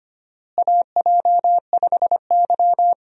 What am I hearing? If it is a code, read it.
Morse code, heard as a keyed tone, about 25 wpm, 700 Hz. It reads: AJ5Y
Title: USA Call Signs - 4 Characters - Repeated 3x - 25wpm